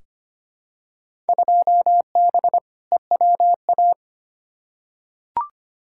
Morse code, heard as a keyed tone, about 25 wpm, 700 Hz. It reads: 2B EWA
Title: Field Day Exchanges - 25wpm